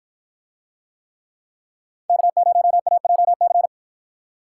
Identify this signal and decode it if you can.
Morse code, heard as a keyed tone, about 53 wpm, 700 Hz. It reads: K0AJC